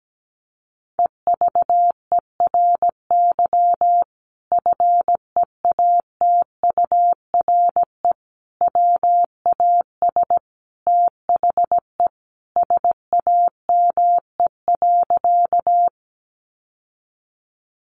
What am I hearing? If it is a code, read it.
Morse code, heard as a keyed tone, about 17 wpm, 700 Hz. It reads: EVERY FEATURE WAS THE SAME.
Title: Sentences from Top 800 Words 17wpm